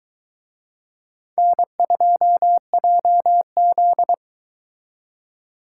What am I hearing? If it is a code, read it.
Morse code, heard as a keyed tone, about 22 wpm, 700 Hz. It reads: N2JZ